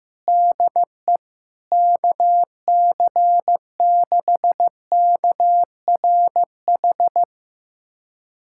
Morse code, heard as a keyed tone, about 15 wpm, 700 Hz. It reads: DE KC6KRH
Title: USA Callsign DE Callsign 15wpm